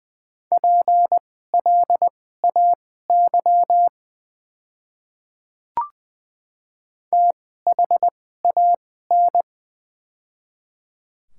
Morse code, heard as a keyed tone, about 20 wpm, spaced as for 15 wpm, 700 Hz. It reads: PLAY THAN
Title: 2x Character Spacing - Top 200 Words - 20wpm